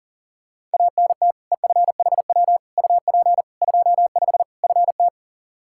Morse code, heard as a keyed tone, about 40 wpm, 700 Hz. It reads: ANT EFHW UP 25 FT
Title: Practice QSO - Exchange 2 - Info - 40wpm